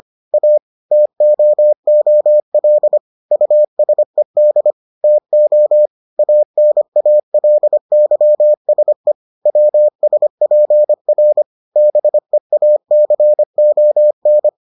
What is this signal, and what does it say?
Morse code, heard as a keyed tone, about 25 wpm, 600 Hz. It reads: A TOOL USED TO ANALYSE WSPR BEACON